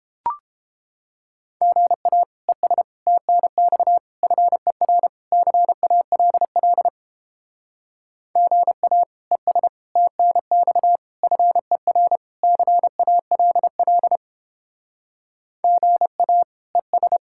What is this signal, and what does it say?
Morse code, heard as a keyed tone, about 31 wpm, 700 Hz. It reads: GA ES TNX FER CALL GA ES TNX FER CALL GA ES